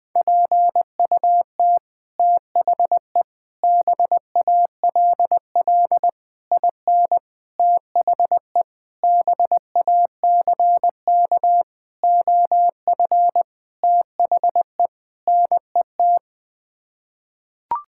Morse code, heard as a keyed tone, about 20 wpm, 700 Hz. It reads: PUT THE BALL IN THE BACK OF THE NET
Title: Sets of 9 Words - 20wpm